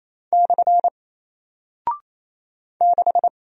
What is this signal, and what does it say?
Morse code, heard as a keyed tone, about 27 wpm, 700 Hz. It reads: / 6
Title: Single Character (Letter:Number:Punctuation:Prosign) Alt 28wpm